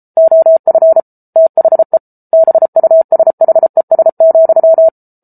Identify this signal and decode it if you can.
Morse code, heard as a keyed tone, about 34 wpm, 650 Hz. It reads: OF THE BUSHES,